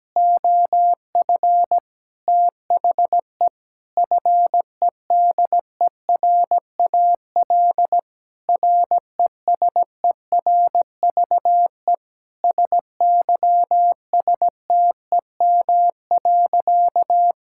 Morse code, heard as a keyed tone, about 17 wpm, 700 Hz. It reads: OF THE FEDERAL RESERVE SYSTEM.
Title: Sentences from Top 1000 Words - Review Part 6 - 17wpm